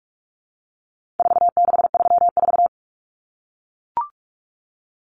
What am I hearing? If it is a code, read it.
Morse code, heard as a keyed tone, about 44 wpm, 700 Hz. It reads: 4634
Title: Numbers - 4-Digit 45wpm